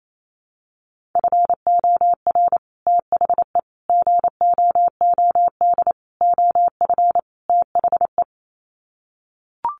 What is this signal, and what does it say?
Morse code, heard as a keyed tone, about 28 wpm, 700 Hz. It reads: FOR THE GOOD OF THE
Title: Sets of 5 Words Encore2 28wpm